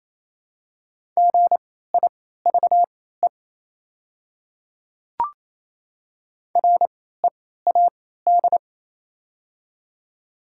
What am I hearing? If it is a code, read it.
Morse code, heard as a keyed tone, about 27 wpm, spaced as for 16 wpm, 700 Hz. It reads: GIVE READ
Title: 3x Character Spacing - Top 200 Words - 28wpm